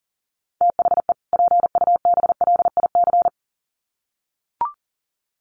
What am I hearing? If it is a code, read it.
Morse code, heard as a keyed tone, about 40 wpm, 700 Hz. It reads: THE PUBLIC
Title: Sets of 2 Words 40wpm